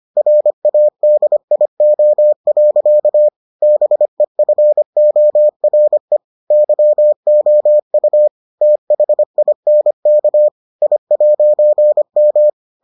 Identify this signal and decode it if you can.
Morse code, heard as a keyed tone, about 25 wpm, 600 Hz. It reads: RADIO. BEFORE YOU THINK I'M